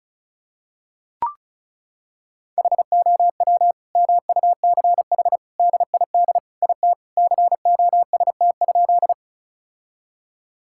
Morse code, heard as a keyed tone, about 36 wpm, 700 Hz. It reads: HOW MUCH DID IT COST?
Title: Sentences from Top 1000 Words - Review Part 5 - 35wpm